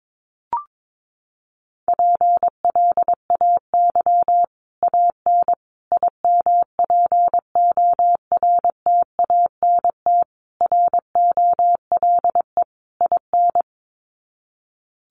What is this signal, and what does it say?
Morse code, heard as a keyed tone, about 22 wpm, 700 Hz. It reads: PLAY AN IMPORTANT ROLE IN
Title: Sets of 5 Words Encore 22wpm